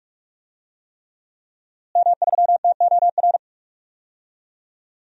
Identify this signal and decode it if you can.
Morse code, heard as a keyed tone, about 46 wpm, 700 Hz. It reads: M3TOR
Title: DXCC-100 Call Signs - 5 Characters - Repeated 3x - 45wpm